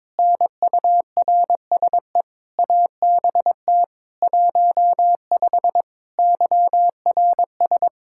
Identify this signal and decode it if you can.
Morse code, heard as a keyed tone, about 22 wpm, 700 Hz. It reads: NURSE ABT 15 YRS
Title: K1USN SST QSO Practice 22wpm